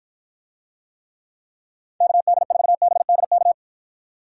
Morse code, heard as a keyed tone, about 52 wpm, 700 Hz. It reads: KD4BDK